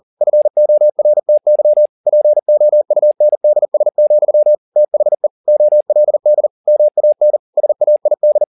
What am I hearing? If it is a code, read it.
Morse code, heard as a keyed tone, about 40 wpm, 600 Hz. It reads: FORTY POUNDS, THE OLD MAN SAID
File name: the old man and the sea 2 40wpm